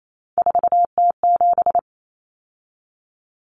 Morse code, heard as a keyed tone, about 28 wpm, 700 Hz. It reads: 4T7
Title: Common Cut Numbers - 3-Digit 28wpm